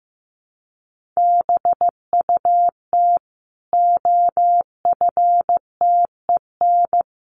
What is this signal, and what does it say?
Morse code, heard as a keyed tone, about 15 wpm, 700 Hz. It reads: BUT OFTEN
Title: Sentences from Top 1400 Words 15wpm